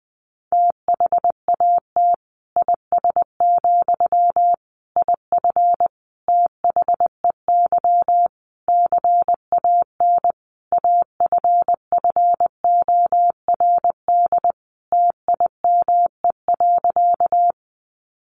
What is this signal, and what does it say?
Morse code, heard as a keyed tone, about 20 wpm, 700 Hz. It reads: THAT IS, IF THEY CAN AFFORD TIME.
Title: Sentences from Top 1100 Words 20wpm